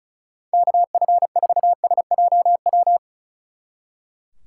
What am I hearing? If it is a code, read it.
Morse code, heard as a keyed tone, about 33 wpm, 700 Hz. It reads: KF4SJW
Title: Speed Racing - USA Call Signs 35wpm